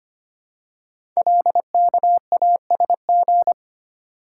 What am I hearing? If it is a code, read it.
Morse code, heard as a keyed tone, about 25 wpm, 700 Hz. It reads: LKASG
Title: Code Groups - 5 Letters - 25wpm